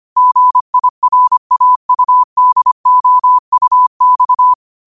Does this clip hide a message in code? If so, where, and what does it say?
Yes: Morse code from 0.2 to 4.5 s, heard as a keyed tone, about 25 wpm, 1000 Hz. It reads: GIRAUDOUX